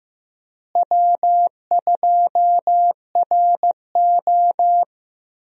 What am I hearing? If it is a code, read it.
Morse code, heard as a keyed tone, about 15 wpm, 700 Hz. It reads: W2RO
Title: USA Call Signs - 4 Characters - Repeated 3x - 15wpm